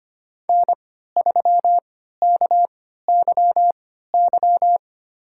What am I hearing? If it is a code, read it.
Morse code, heard as a keyed tone, about 25 wpm, spaced as for 14 wpm, 700 Hz. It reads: N3KYY
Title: USA Call Signs - 5 Characters - 3x Character Spacing - 25wpm